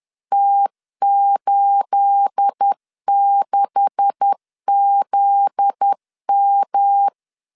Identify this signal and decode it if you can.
Morse code, heard as a keyed tone, about 10 wpm, 800 Hz. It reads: T86ZM
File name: LEZIONI / 9 2867 10wpm 2733